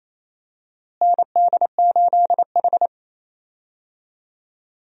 Morse code, heard as a keyed tone, about 28 wpm, 700 Hz. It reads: ND8H
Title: USA Call Signs - 4 Characters - Repeated 3x - 28wpm